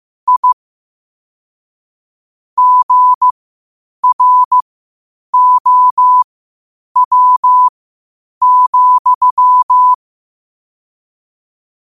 Morse code, heard as a keyed tone, about 15 wpm, spaced as for 8 wpm, 1000 Hz. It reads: I GROW,
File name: qotdfiles / 8wpm